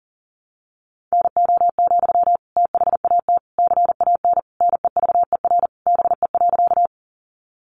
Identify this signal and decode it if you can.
Morse code, heard as a keyed tone, about 40 wpm, 700 Hz. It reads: NO, THAT CAN NEVER BE.
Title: Sentences from Top 300 Words 40wpm